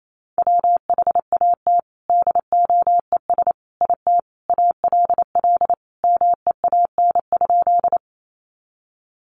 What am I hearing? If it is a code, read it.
Morse code, heard as a keyed tone, about 28 wpm, 700 Hz. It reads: WHAT DOES IT ALL MEAN?
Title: Sentences from Top 200 Words 28wpm